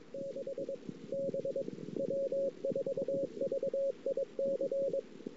Morse code, heard as a keyed tone, about 23 wpm, 500 Hz, with noise. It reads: 6 B W4VIC
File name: CW / 2008-ss-cw-k5tr-op-n5rz -19:00